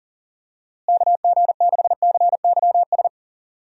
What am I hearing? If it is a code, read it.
Morse code, heard as a keyed tone, about 40 wpm, 700 Hz. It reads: KG6CYS
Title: USA Call Signs - 6 Characters - Repeated 2x - 40wpm